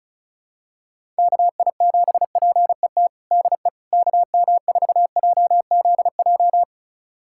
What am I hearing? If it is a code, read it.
Morse code, heard as a keyed tone, about 36 wpm, 700 Hz. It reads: KI7PET DE KM4JZJ
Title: USA Callsign DE Callsign - 6 Characters - 35wpm